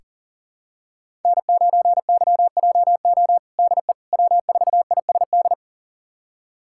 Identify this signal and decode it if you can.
Morse code, heard as a keyed tone, about 40 wpm, 700 Hz. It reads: N9YJO DE W4ISD
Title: USA Callsign DE Callsign - 5 Characters - 40wpm